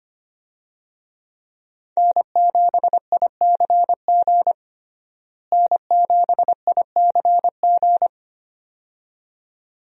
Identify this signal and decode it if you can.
Morse code, heard as a keyed tone, about 25 wpm, 700 Hz. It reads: N7ICG N7ICG
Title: USA Call Signs - 5 Characters - Repeated 2x - 25wpm